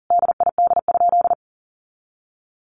Morse code, heard as a keyed tone, about 37 wpm, 700 Hz. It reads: DID?
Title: Sentences from Top 100 Words 40wpm